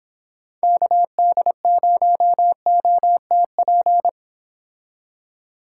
Morse code, heard as a keyed tone, about 29 wpm, 700 Hz. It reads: KD0OTP